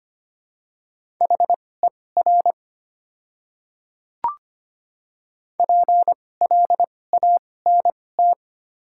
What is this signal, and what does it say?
Morse code, heard as a keyed tone, about 25 wpm, spaced as for 18 wpm, 700 Hz. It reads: HER PLANT
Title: 2x Character Spacing - Top 500 Words - 25wpm